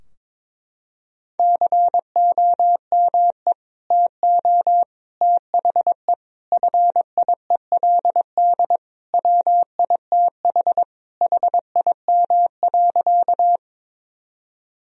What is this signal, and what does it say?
Morse code, heard as a keyed tone, about 22 wpm, 700 Hz. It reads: COME TO THE FIELD WITH HIM.
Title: Sentences from Top 1000 Words - Review Part 2 - 22wpm